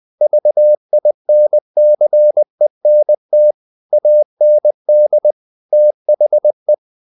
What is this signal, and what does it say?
Morse code, heard as a keyed tone, about 20 wpm, 600 Hz. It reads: VINCENT AND THE
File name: CW / Practice / NationalCapitals20